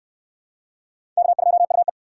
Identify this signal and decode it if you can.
Morse code, heard as a keyed tone, about 61 wpm, 700 Hz. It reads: K2UE